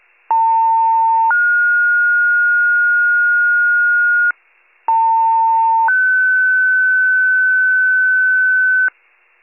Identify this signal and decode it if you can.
Two-tone page, heard as a one-second tone, then a three-second tone, about 900 then 1430 Hz, about 900 then 1500 Hz. It is heard from 0.3 to 8.9 s.